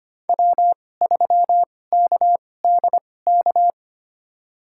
Morse code, heard as a keyed tone, about 25 wpm, spaced as for 18 wpm, 700 Hz. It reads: W3KDK
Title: USA Call Signs - 5 Characters - 2x Character Spacing - 25wpm